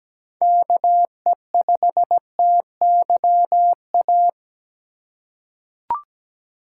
Morse code, heard as a keyed tone, about 17 wpm, 700 Hz. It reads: KE5TYA
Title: USA Call Signs - 6 Characters - 3x Character Spacing - 17wpm